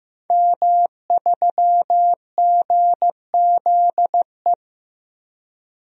Morse code, heard as a keyed tone, about 15 wpm, 700 Hz. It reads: M3GZE
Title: DXCC-100 Call Signs - 5 Characters - Repeated 2x - 15wpm